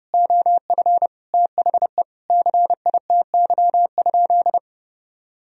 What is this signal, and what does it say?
Morse code, heard as a keyed tone, about 32 wpm, 700 Hz. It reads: OF THE CITY?
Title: Sentences from Top 1000 Words - Review Part 2 - 30wpm